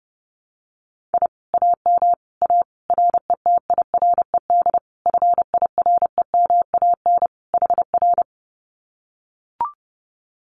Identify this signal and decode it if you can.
Morse code, heard as a keyed tone, about 30 wpm, 700 Hz. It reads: I AM A RETIRED FIREMAN HR
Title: Common QSO Phrases Encore 30wpm